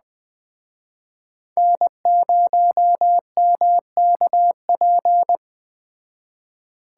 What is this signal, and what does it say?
Morse code, heard as a keyed tone, about 20 wpm, 700 Hz. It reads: N0MKP
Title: Speed Racing - USA 1x3 Call Signs 17wpm